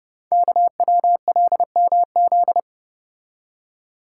Morse code, heard as a keyed tone, about 30 wpm, 700 Hz. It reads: KWLMZ
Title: Code Groups - 5 Letter:Numbers - 30wpm